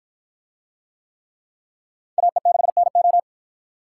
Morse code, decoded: AE6NO

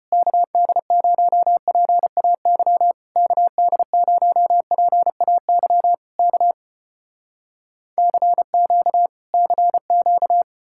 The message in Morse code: KD0PAY KD0PAY K CQ CQ

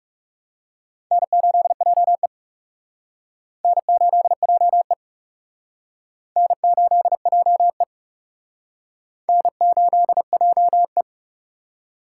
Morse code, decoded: N8JE N8JE N8JE N8JE

